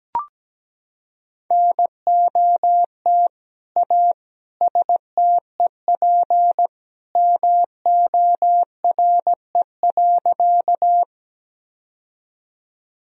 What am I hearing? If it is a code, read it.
Morse code, heard as a keyed tone, about 17 wpm, 700 Hz. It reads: NOT A STEP MORE.
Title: Sentences from Top 1000 Words - Review Part 2 - 17wpm